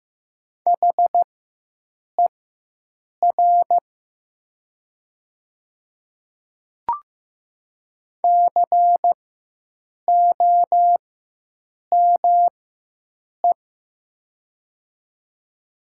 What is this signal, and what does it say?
Morse code, heard as a keyed tone, about 15 wpm, spaced as for 7 wpm, 700 Hz. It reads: HER COME
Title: 4x Character Spacing - Top 100 Words - 15wpm